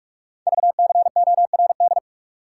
Morse code, heard as a keyed tone, about 44 wpm, 700 Hz. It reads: VCORD
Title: Code Groups - 5 Letters - 45wpm